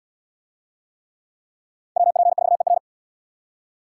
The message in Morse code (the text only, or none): WP4ER